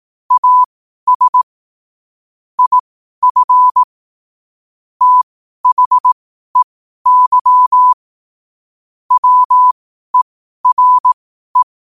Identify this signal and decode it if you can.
Morse code, heard as a keyed tone, about 18 wpm, spaced as for 12 wpm, 1000 Hz. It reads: AS IF THEY WERE
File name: qotdfiles / 12wpm